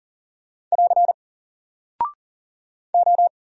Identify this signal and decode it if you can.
Morse code, heard as a keyed tone, about 40 wpm, 700 Hz. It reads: <AR> O
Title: Single Character (Letter:Number:Punctuation:Prosign) Alt 40wpm